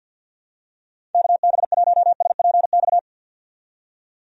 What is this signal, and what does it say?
Morse code, heard as a keyed tone, about 50 wpm, 700 Hz. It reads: KB1SPX
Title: USA Call Signs - 6 Characters - 50wpm